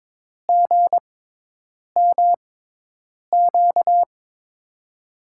Morse code, heard as a keyed tone, about 22 wpm, spaced as for 8 wpm, 700 Hz. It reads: GMQ
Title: USA Call Signs - 6 Characters - 6x Character Spacing - 22wpm